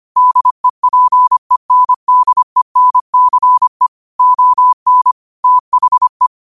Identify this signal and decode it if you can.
Morse code, heard as a keyed tone, about 25 wpm, 1000 Hz. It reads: DEPENDENCE ON THE
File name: qotdfiles / 25wpm